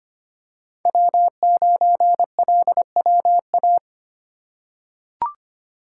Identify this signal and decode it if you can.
Morse code, heard as a keyed tone, about 25 wpm, 700 Hz. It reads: W9LWA
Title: USA Call Signs - 5 Characters - 2x Character Spacing - 25wpm